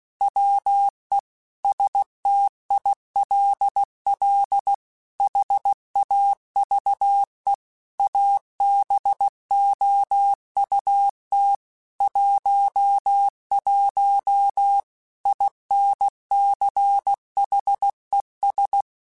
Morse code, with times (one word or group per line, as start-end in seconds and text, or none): 0.2-1.2 WE
1.6-4.7 STILL
5.2-7.5 HAVE
8.0-11.6 ABOUT
12.0-14.8 11
15.3-18.8 INCHES